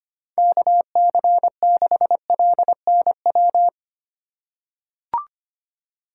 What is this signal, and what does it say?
Morse code, heard as a keyed tone, about 25 wpm, 700 Hz. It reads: KC6LNW
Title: USA Call Signs - 6 Characters - 4x Character Spacing - 25wpm